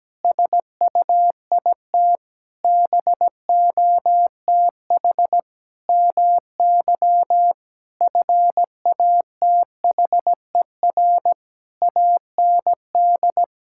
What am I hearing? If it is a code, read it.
Morse code, heard as a keyed tone, about 17 wpm, 700 Hz. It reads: SUIT BOTH MY FATHER AND
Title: Sentences from Top 1600 Words 17wpm